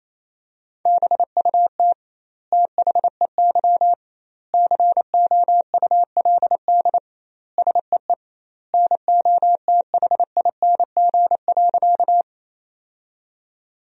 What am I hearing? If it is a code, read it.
Morse code, heard as a keyed tone, about 28 wpm, 700 Hz. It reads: BUT THEY COULD SEE NOTHING.
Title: 2x Word-Spacing - Sentences from Top 500 Words - 28wpm